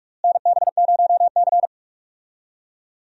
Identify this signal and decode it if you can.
Morse code, heard as a keyed tone, about 41 wpm, 700 Hz. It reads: NB0C
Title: USA Call Signs - 4 Characters - Repeated 3x - 45wpm